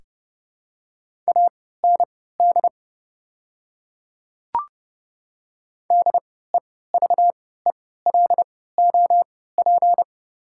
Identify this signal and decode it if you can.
Morse code, heard as a keyed tone, about 30 wpm, spaced as for 17 wpm, 690 Hz. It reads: AND DEVELOP
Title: 3x Character Spacing - Top 500 Words - 30wpm